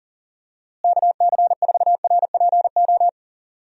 Morse code, heard as a keyed tone, about 40 wpm, 700 Hz. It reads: KC4RPO
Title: USA Call Signs - 6 Characters - Repeated 3x - 40wpm